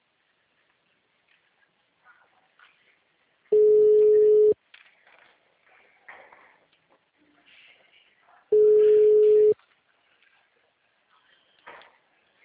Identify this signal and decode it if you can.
Ringback tone (Europe, 425 Hz).